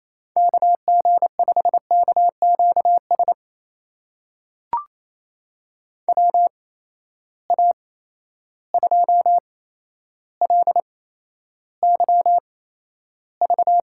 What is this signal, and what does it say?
Morse code, heard as a keyed tone, about 28 wpm, spaced as for 12 wpm, 700 Hz. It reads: KG5KQS WA2LYV
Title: USA Call Signs - 6 Characters - 8x Character Spacing - 28wpm